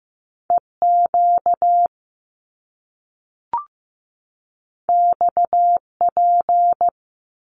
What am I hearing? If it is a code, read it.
Morse code, heard as a keyed tone, about 15 wpm, 700 Hz. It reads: EQ XP
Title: Top Two Letter Combinations in Words 15wpm